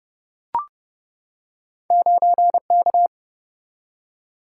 Morse code, heard as a keyed tone, about 31 wpm, 700 Hz. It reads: 9K
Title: Single-Letter Single-Number 30wpm